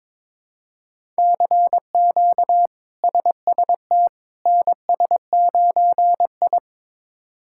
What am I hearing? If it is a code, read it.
Morse code, heard as a keyed tone, about 22 wpm, 700 Hz. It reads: CQ SST NS9I
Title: K1USN SST QSO Practice - 22wpm